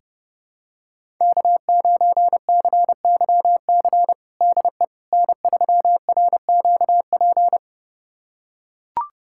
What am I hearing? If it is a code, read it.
Morse code, heard as a keyed tone, about 30 wpm, 700 Hz. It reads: K9CYC DE N3RQP
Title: USA Callsign DE Callsign - 5 Characters - 30wpm